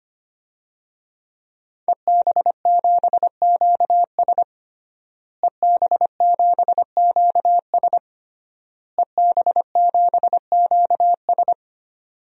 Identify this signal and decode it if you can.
Morse code, heard as a keyed tone, about 25 wpm, 700 Hz. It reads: EB7QS EB7QS EB7QS
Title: DXCC-100 Call Signs - 5 Characters - Repeated 3x - 25wpm